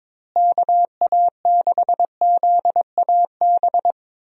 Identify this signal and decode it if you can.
Morse code, heard as a keyed tone, about 22 wpm, 700 Hz. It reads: KA6ZAB
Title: USA Call Signs - 6 Characters - Repeated 2x - 22wpm